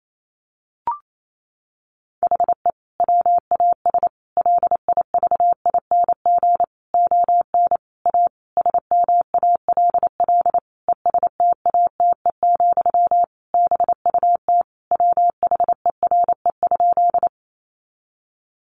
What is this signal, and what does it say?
Morse code, heard as a keyed tone, about 28 wpm, 700 Hz. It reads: HE WAS LIVING ON A SMALL ESTATE, BUT WHERE?
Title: Sentences from Top 1200 Words 28wpm